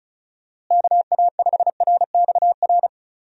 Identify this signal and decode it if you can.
Morse code, heard as a keyed tone, about 36 wpm, 700 Hz. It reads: KA5RXR
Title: USA Call Signs - 6 Characters - Repeated 3x - 35wpm